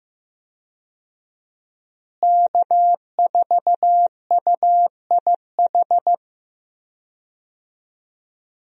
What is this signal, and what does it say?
Morse code, heard as a keyed tone, about 15 wpm, 700 Hz. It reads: K4UIH
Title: USA Call Signs - 5 Characters - Repeated 2x - 15wpm